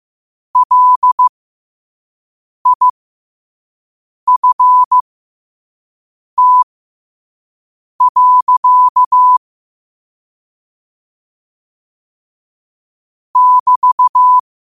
Morse code, heard as a keyed tone, about 15 wpm, spaced as for 5 wpm, 1000 Hz. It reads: LIFT. <BT>
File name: qotdfiles / 5wpm